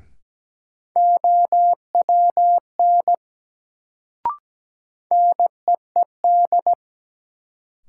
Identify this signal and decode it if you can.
Morse code, heard as a keyed tone, about 17 wpm, 700 Hz. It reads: OWN NEED
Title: Sets of 2 Words - New Word Drill - 17wpm